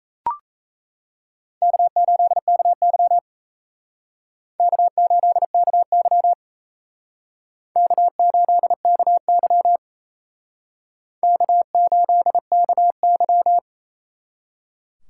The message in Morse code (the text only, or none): K8KY K8KY K8KY K8KY